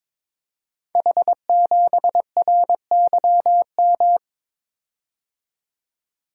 Morse code, heard as a keyed tone, about 22 wpm, 700 Hz. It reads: H7RYM
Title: Code Groups - 5 Letter:Numbers - 22wpm